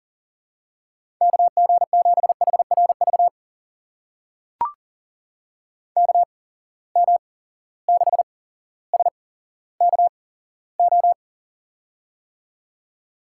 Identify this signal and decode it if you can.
Morse code, heard as a keyed tone, about 40 wpm, spaced as for 17 wpm, 700 Hz. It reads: KG7HRV KM6SKO
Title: USA Call Signs - 6 Characters - 8x Character Spacing - 40wpm